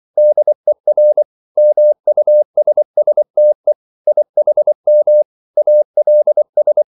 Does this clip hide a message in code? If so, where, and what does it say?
Yes: Morse code from 0.2 to 6.8 s, heard as a keyed tone, about 24 wpm, 600 Hz. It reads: DER MUSSTE IHM ALS